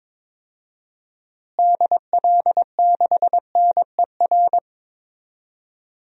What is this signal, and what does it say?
Morse code, heard as a keyed tone, about 22 wpm, 700 Hz. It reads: DL6NER
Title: DXCC-100 Call Signs - 6 Characters - Repeated 3x - 22wpm